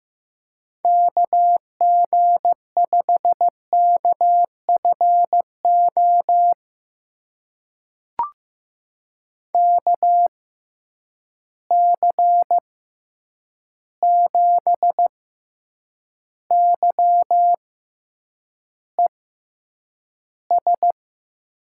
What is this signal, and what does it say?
Morse code, heard as a keyed tone, about 15 wpm, spaced as for 8 wpm, 700 Hz. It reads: KG5KFO KC7YES
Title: USA Call Signs - 6 Characters - 6x Character Spacing - 15wpm